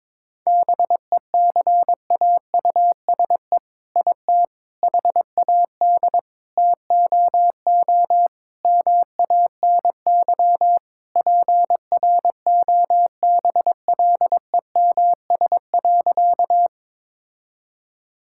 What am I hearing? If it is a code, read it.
Morse code, heard as a keyed tone, about 22 wpm, 700 Hz. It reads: BECAUSE IT HAD TOO MANY PROBLEMS.
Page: morse-code-ninja-practice-22wpm